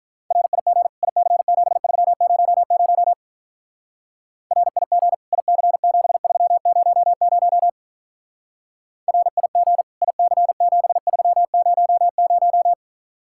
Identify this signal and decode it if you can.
Morse code, heard as a keyed tone, about 44 wpm, 700 Hz. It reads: RIG IC7300 RIG IC7300 RIG IC7300
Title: Speed-Racing Common QSO Phrases - 35wpm